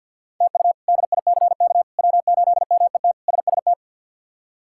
Morse code, heard as a keyed tone, about 47 wpm, 700 Hz. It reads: TU DICK W8MET SST
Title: K1USN SST QSO Practice - 50wpm